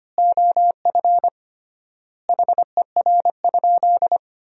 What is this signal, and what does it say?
Morse code, heard as a keyed tone, about 25 wpm, 700 Hz. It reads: OF HER?
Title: 3x Word-Spacing - Sentences from Top 200 Words - 25wpm